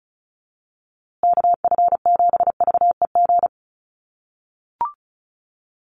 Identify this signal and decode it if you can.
Morse code, heard as a keyed tone, about 34 wpm, 700 Hz. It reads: KF7VEG
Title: USA Call Signs Repeated 2x 35wpm